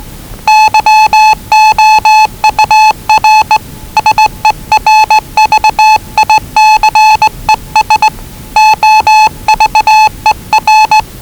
Morse code, heard as a keyed tone, about 19 wpm, 840 Hz. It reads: YOUR SERVICES OVER
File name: WellingtonRadioZlwCloseDownOnHfAndMf / 01-Zlw 4277khz 417.5 qrt until 0925z